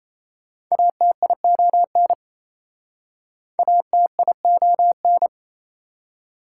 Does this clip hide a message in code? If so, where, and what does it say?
Yes: Morse code from 0.7 to 5.3 s, heard as a keyed tone, about 36 wpm, 700 Hz. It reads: ATION ATION